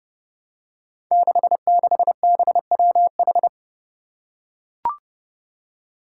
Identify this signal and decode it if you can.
Morse code, heard as a keyed tone, about 30 wpm, 700 Hz. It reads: 66BWH
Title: Five Challenging Letter-Number Combinations 30wpm